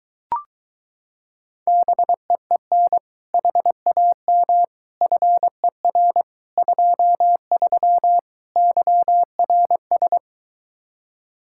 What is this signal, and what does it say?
Morse code, heard as a keyed tone, about 24 wpm, 700 Hz. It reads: BEEN HAM FER 23 YRS